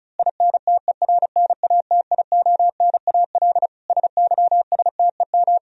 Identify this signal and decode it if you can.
Morse code, heard as a keyed tone, about 34 wpm, 700 Hz. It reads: INTERNATIONAL SYSTEM
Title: Sentences from Top 1000 Words - Review Part 4 - 35wpm